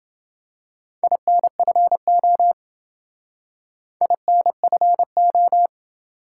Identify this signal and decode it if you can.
Morse code, heard as a keyed tone, about 28 wpm, 700 Hz. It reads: INFO INFO